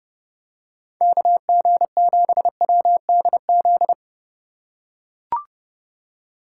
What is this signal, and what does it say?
Morse code, heard as a keyed tone, about 30 wpm, 700 Hz. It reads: KG7WDZ
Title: USA Call Signs - 6 Characters - 2x Character Spacing - 30wpm